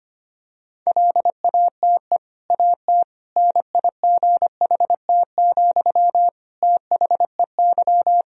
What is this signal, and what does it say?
Morse code, heard as a keyed tone, about 25 wpm, 700 Hz. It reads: LATE AT NIGHT, THEY
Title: Sentences from Top 1000 Words - Review Part 3 - 25wpm